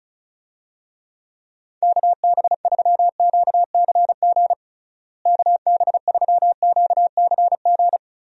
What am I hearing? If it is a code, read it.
Morse code, heard as a keyed tone, about 36 wpm, 700 Hz. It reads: KB3QCG KB3QCG
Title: USA Call Signs - 6 Characters - Repeated 3x - 35wpm